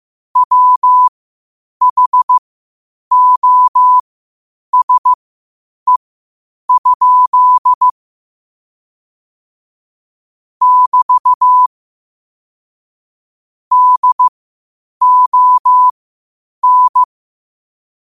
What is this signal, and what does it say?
Morse code, heard as a keyed tone, about 15 wpm, spaced as for 8 wpm, 1000 Hz. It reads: WHOSE? <BT> DON
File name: qotdfiles / 8wpm